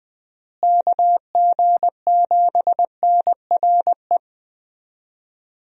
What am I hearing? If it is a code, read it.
Morse code, heard as a keyed tone, about 21 wpm, 700 Hz. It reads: KG7NRE
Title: USA Call Signs - 6 Characters - Repeated 3x - 20wpm